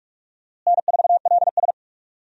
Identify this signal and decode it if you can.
Morse code, heard as a keyed tone, about 47 wpm, 700 Hz. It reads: N4LS
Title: USA Call Signs - 4 Characters - Repeated 3x - 45wpm